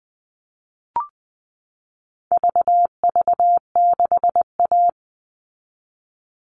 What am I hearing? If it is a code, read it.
Morse code, heard as a keyed tone, about 20 wpm, 700 Hz. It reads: VV6A
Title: All Cut Numbers - 4-Digit 20wpm